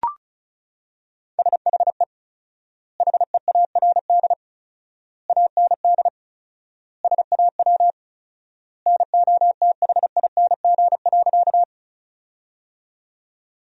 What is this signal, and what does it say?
Morse code, heard as a keyed tone, about 36 wpm, 700 Hz. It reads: SHE HEARD AND SAW NOTHING.
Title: 4x Word-Spacing - Sentences from Top 500 Words - 35wpm